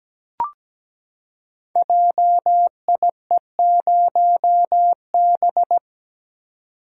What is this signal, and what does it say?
Morse code, heard as a keyed tone, about 17 wpm, 700 Hz. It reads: JIE0B
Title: Code Groups - 5 Letter:Numbers - 17wpm